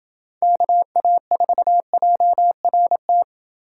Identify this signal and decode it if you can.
Morse code, heard as a keyed tone, about 27 wpm, 700 Hz. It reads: KA4JRT